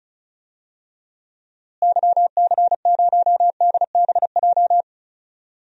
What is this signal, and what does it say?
Morse code, heard as a keyed tone, about 34 wpm, 700 Hz. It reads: YC0DBJ
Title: DXCC-100 Call Signs - 6 Characters - Repeated 2x - 35wpm